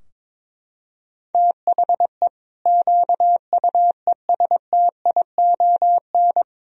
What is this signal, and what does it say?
Morse code, heard as a keyed tone, about 21 wpm, 700 Hz. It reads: THE QUESTION
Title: Sentences from Top 1000 Words - Review Part 3 - 22wpm